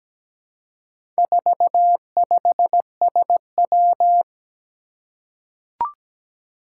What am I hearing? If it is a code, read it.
Morse code, heard as a keyed tone, about 17 wpm, 700 Hz. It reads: 45SW
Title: Four Challenging Letter-Number Combinations 17wpm